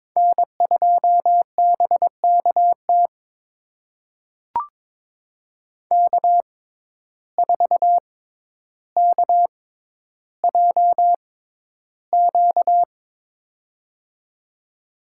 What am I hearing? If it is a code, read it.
Morse code, heard as a keyed tone, about 22 wpm, spaced as for 11 wpm, 700 Hz. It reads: N2BKT K4KJQ
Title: USA Call Signs - 5 Characters - 6x Character Spacing - 22wpm